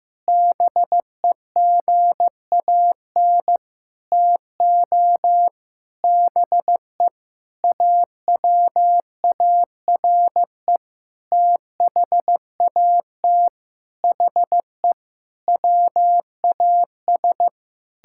Morse code, heard as a keyed tone, about 15 wpm, 700 Hz. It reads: BEGAN TO BE AWARE THAT HE WAS
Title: Sentences from Top 1600 Words 15wpm